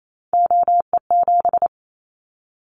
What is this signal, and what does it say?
Morse code, heard as a keyed tone, about 28 wpm, 700 Hz. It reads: OE7
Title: Callsign Prefixes for Top 50 DXCC - 28wpm